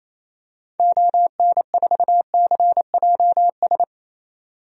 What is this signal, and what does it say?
Morse code, heard as a keyed tone, about 28 wpm, 700 Hz. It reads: ON4CJS
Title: DXCC-100 Call Signs - 6 Characters - 28wpm